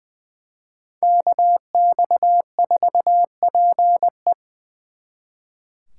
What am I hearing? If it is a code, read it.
Morse code, heard as a keyed tone, about 23 wpm, 700 Hz. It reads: KX4PE
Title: Speed Racing - USA Call Signs 20wpm